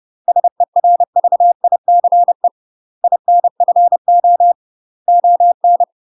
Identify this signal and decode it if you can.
Morse code, heard as a keyed tone, about 30 wpm, 700 Hz. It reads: SERVICE INFO ON
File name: UK MASTER QSOs TRAINING / 30WPM / UK MASTER QSOs 0034